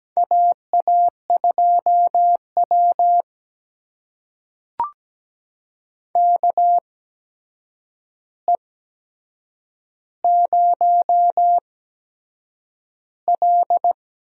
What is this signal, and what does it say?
Morse code, heard as a keyed tone, about 17 wpm, spaced as for 7 wpm, 700 Hz. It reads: AA2W KE0L
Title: USA Call Signs - 8x Character Spacing - 17wpm